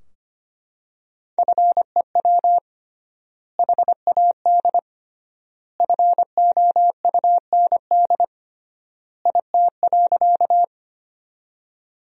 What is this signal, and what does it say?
Morse code, heard as a keyed tone, about 25 wpm, 700 Hz. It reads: FEW HAD FOUND IT.
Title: 3x Word-Spacing - Sentences from Top 300 Words - 25wpm